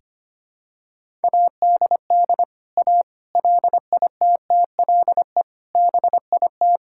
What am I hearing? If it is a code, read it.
Morse code, heard as a keyed tone, about 25 wpm, 700 Hz. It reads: ADD A LITTLE BIT
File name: practice / 25wpm / Sentences with 4-Word Phrases - 25wpm